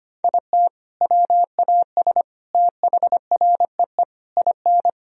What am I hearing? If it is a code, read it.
Morse code, heard as a keyed tone, about 25 wpm, 700 Hz. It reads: IT WAS THREE IN
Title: Sentences from Top 1000 Words - Review Part 1 - 25wpm